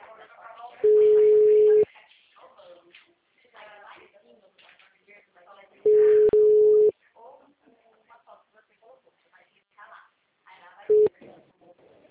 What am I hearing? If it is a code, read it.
Ringback tone (Europe, 425 Hz), with speech.